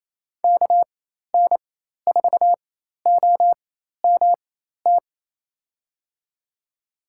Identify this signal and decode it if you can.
Morse code, heard as a keyed tone, about 28 wpm, spaced as for 13 wpm, 700 Hz. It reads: KN4OMT
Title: USA Call Signs - 6 Characters - 4x Character Spacing - 28wpm